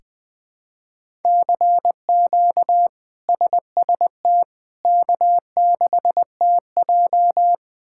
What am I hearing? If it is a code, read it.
Morse code, heard as a keyed tone, about 21 wpm, 700 Hz. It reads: CQ SST K6TJ
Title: K1USN SST QSO Practice - 20wpm